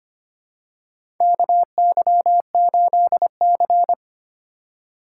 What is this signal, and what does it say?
Morse code, heard as a keyed tone, about 25 wpm, 700 Hz. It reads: KY8C
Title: USA Call Signs - 4 Characters - 4x Character Spacing - 25wpm